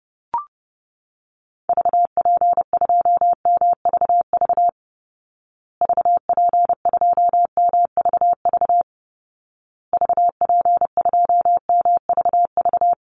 Morse code, heard as a keyed tone, about 30 wpm, 700 Hz. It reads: VP2MVV VP2MVV VP2MVV
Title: Call Signs Repeated 2x - 30wpm